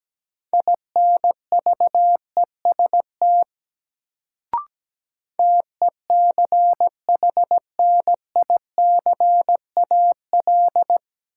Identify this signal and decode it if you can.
Morse code, heard as a keyed tone, about 17 wpm, 700 Hz. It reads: INVEST TECHNICAL
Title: Sentences from Top 1100 Words - New Word Drill Alt - 17wpm